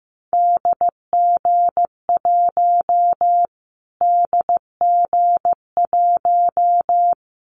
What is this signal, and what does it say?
Morse code, heard as a keyed tone, about 15 wpm, 700 Hz. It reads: DG1 DG1
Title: Callsign Prefixes for Top 25 DXCC - Repeated 3x - 15wpm